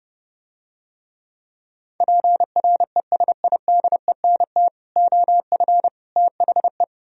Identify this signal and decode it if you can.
Morse code, heard as a keyed tone, about 29 wpm, 700 Hz. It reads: PRESIDENT OF THE